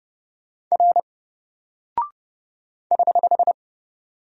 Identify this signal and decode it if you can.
Morse code, heard as a keyed tone, about 30 wpm, 700 Hz. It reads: R <HH>
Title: Single Character (Letter:Number:Punctuation:Prosign) 30wpm